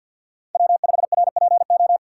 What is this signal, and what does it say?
Morse code, heard as a keyed tone, about 49 wpm, 700 Hz. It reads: W5RPO
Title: USA Call Signs - 5 Characters - 4x Character Spacing - 50wpm